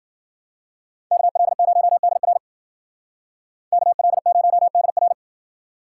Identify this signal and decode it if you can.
Morse code, heard as a keyed tone, about 48 wpm, 700 Hz. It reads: KL0DR KL0DR
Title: Speed Racing - USA 2x2 Call Signs 40wpm